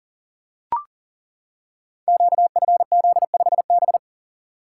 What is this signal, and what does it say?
Morse code, heard as a keyed tone, about 38 wpm, 700 Hz. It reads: QFZ5B
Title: Code Groups - 5 Letter:Numbers - 40wpm